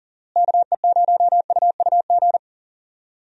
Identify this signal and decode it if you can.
Morse code, heard as a keyed tone, about 40 wpm, 700 Hz. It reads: KE0UUG
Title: USA Call Signs - 6 Characters - Repeated 2x - 40wpm